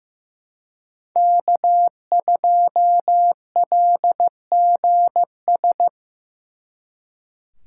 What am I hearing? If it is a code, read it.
Morse code, heard as a keyed tone, about 18 wpm, 700 Hz. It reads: K2LGS
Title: Speed Racing - USA 1x3 Call Signs 15wpm